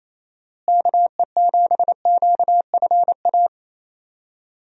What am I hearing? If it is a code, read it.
Morse code, heard as a keyed tone, about 28 wpm, 700 Hz. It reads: KE7QFA
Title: USA Call Signs - 6 Characters - Repeated 2x - 28wpm